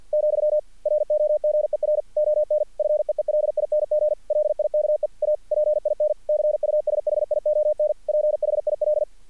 Morse code, heard as a keyed tone, about 46 wpm, 600 Hz. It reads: , ROGER ON FEELING LIKE A PIN CUSHION PHIL
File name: archived / CRSnet